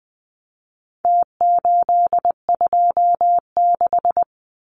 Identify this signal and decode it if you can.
Morse code, heard as a keyed tone, about 20 wpm, 700 Hz. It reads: T826
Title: Numbers - Contest Style - 4-Digit 20wpm